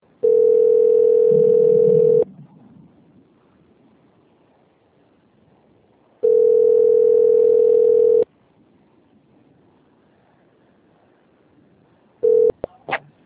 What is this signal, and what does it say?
Ringback tone (North America), 440 + 480 Hz.